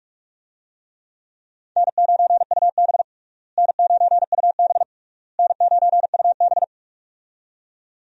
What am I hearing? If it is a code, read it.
Morse code, heard as a keyed tone, about 46 wpm, 700 Hz. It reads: N9UB N9UB N9UB
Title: USA Call Signs - 4 Characters - Repeated 3x - 45wpm